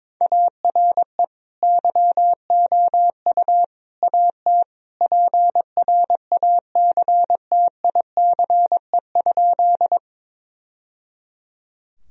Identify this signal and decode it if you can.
Morse code, heard as a keyed tone, about 22 wpm, 700 Hz. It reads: ARE YOU AT PRACTICE?
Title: Sentences from Top 1000 Words - Review Part 4 - 22wpm